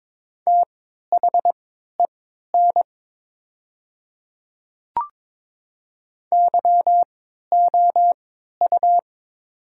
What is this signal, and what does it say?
Morse code, heard as a keyed tone, about 22 wpm, spaced as for 12 wpm, 700 Hz. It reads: THEN YOU
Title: 3x Character Spacing - Top 100 Words - 22wpm